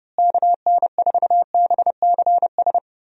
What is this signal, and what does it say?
Morse code, heard as a keyed tone, about 32 wpm, 700 Hz. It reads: KN4BCS
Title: USA Call Signs - 6 Characters - Repeated 2x - 30wpm